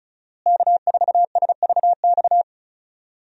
Morse code, heard as a keyed tone, about 36 wpm, 700 Hz. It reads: K4SVX
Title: USA Call Signs - 5 Characters - Repeated 3x - 35wpm